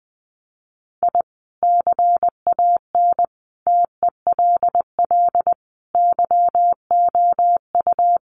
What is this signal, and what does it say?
Morse code, heard as a keyed tone, about 20 wpm, 700 Hz. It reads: I CAN TELL YOU
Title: Sets of 5 Words 20wpm